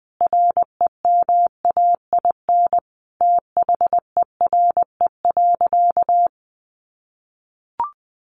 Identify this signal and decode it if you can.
Morse code, heard as a keyed tone, about 20 wpm, 700 Hz. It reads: REMAIN THERE.
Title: Sentences from Top 1200 Words 20wpm